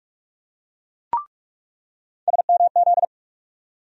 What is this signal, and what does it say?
Morse code, heard as a keyed tone, about 42 wpm, 700 Hz. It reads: SMZ
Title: USA Call Sign Postfixes - 3 Letters 45wpm